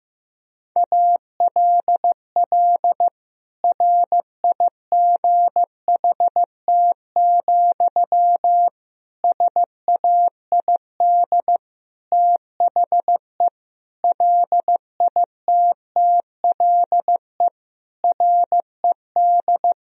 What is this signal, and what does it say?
Morse code, heard as a keyed tone, about 15 wpm, 700 Hz. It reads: ALL RIGHT, SAID THE LITTLE RED
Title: Sentences from Top 1000 Words - Review Part 2 - 15wpm